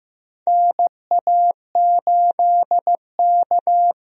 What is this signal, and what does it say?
Morse code, heard as a keyed tone, about 15 wpm, 700 Hz. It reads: NA8K
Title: USA Call Signs - 4 Characters - Repeated 3x - 15wpm